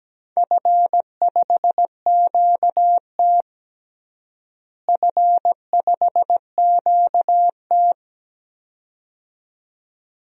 Morse code, heard as a keyed tone, about 17 wpm, 700 Hz. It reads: F5QT F5QT
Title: DXCC-100 Call Signs - 4 Characters - Repeated 2x - 17wpm